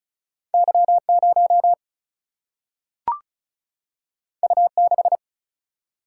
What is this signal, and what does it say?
Morse code, heard as a keyed tone, about 35 wpm, 700 Hz. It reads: Y0 U6
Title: Single-Letter Single-Number 35wpm